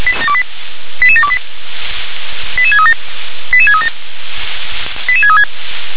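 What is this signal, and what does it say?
Five-tone selective call (ZVEI), heard as quick runs of five tones, tones of about 70 ms. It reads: 88728 88637 88637 88636
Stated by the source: ZVEI five-tone selective call, code 88915, received by SDR.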